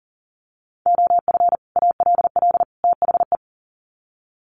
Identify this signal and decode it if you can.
Morse code, heard as a keyed tone, about 40 wpm, 700 Hz. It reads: OF ALL THE
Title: Sets of 3 Words 40wpm